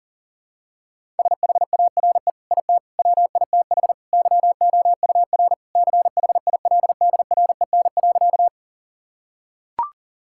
Morse code, heard as a keyed tone, about 40 wpm, 700 Hz. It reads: SHARE IT WITH YOUR CHILDREN.
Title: Sentences from Top 1000 Words - Review Part 6 - 40wpm